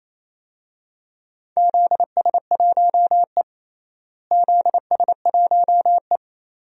Morse code, heard as a keyed tone, about 28 wpm, 700 Hz. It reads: ZS1E ZS1E
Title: DXCC-100 Call Signs - 4 Characters - Repeated 3x - 28wpm